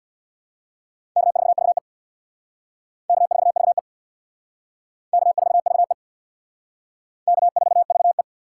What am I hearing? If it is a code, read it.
Morse code, heard as a keyed tone, about 59 wpm, 700 Hz. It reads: K4VE K4VE K4VE K4VE
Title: Speed Racing - USA 1x2 Call Signs 50wpm